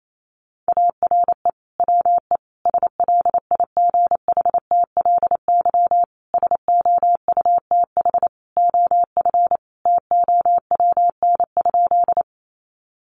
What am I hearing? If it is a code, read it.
Morse code, heard as a keyed tone, about 28 wpm, 700 Hz. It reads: ARE WE SLIGHTLY SOUTH OF TOWN?
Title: Sentences from Top 1300 Words 28wpm